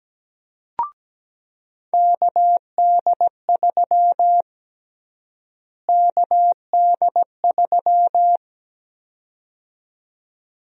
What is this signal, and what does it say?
Morse code, heard as a keyed tone, about 17 wpm, 700 Hz. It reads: KD3 KD3
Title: USA Call Sign Prefixes - Repeated 2x - 17wpm